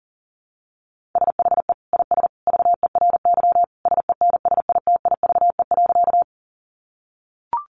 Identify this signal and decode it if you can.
Morse code, heard as a keyed tone, about 37 wpm, 700 Hz. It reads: SHE IS VERY SENSITIVE.
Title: Sentences from Top 1400 Words 40wpm